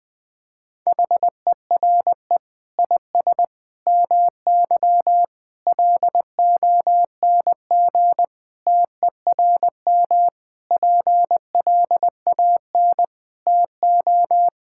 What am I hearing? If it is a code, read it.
Morse code, heard as a keyed tone, about 20 wpm, 700 Hz. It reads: HERE IS MY LONG TERM PLAN TO
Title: Sentences from Top 1600 Words 20wpm